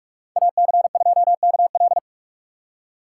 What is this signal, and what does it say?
Morse code, heard as a keyed tone, about 46 wpm, 700 Hz. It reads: AC2KL
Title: USA Call Signs - 4x Character Spacing - 45wpm